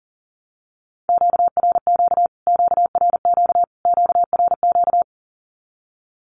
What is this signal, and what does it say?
Morse code, heard as a keyed tone, about 40 wpm, 700 Hz. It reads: QRQ QRQ QRQ
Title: Q-Codes Repeated 3x 40wpm